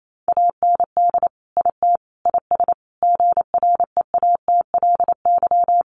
Morse code, heard as a keyed tone, about 28 wpm, 700 Hz. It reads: AND IT IS GREATLY
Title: Sentences from Top 1400 Words 28wpm